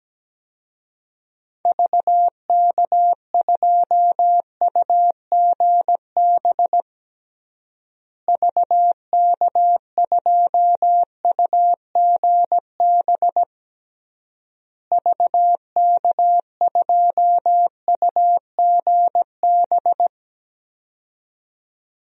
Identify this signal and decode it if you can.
Morse code, heard as a keyed tone, about 17 wpm, 700 Hz. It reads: VK2UGB VK2UGB VK2UGB
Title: DXCC-100 Call Signs - 6 Characters - Repeated 3x - 17wpm